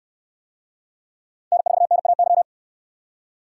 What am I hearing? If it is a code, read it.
Morse code, heard as a keyed tone, about 66 wpm, 700 Hz. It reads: N4NAY